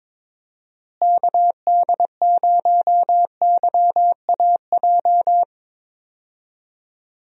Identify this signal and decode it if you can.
Morse code, heard as a keyed tone, about 21 wpm, 700 Hz. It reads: KD0YAJ
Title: USA Call Signs - 6 Characters - Repeated 3x - 22wpm